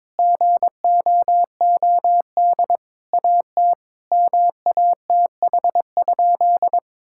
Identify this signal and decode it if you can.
Morse code, heard as a keyed tone, about 22 wpm, 700 Hz. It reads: GOOD AT MATH?
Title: Sentences from Top 1000 Words - Review Part 6 - 22wpm